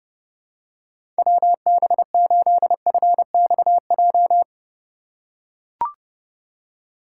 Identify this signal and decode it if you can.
Morse code, heard as a keyed tone, about 30 wpm, 700 Hz. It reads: WB8FXJ
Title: K1USN SST QSO Practice - 30wpm